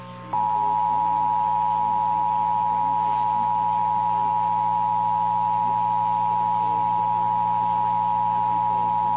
Emergency Alert System (SAME) tones (the 853 and 960 Hz attention tone), with noise. It is heard from 0.7 to 8.1 s.
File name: eas / KNCYKBIE / KNCYKBIE